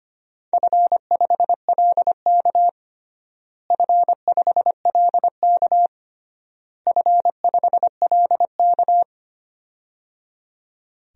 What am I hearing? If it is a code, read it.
Morse code, heard as a keyed tone, about 25 wpm, 700 Hz. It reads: F5LK F5LK F5LK
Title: DXCC-100 Call Signs - 4 Characters - Repeated 3x - 25wpm